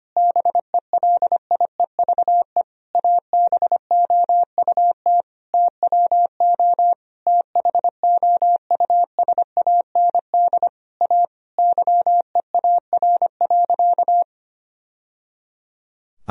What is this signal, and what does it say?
Morse code, heard as a keyed tone, about 25 wpm, 700 Hz. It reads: BELIEVE ABOUT TWO THOUSAND A YEAR.
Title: Sentences from Top 1000 Words - Review Part 3 - 25wpm